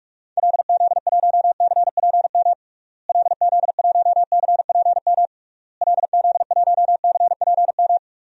Morse code, heard as a keyed tone, about 45 wpm, 700 Hz. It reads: LZ1CPM LZ1CPM LZ1CPM
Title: DXCC-100 Call Signs - 6 Characters - Repeated 3x - 45wpm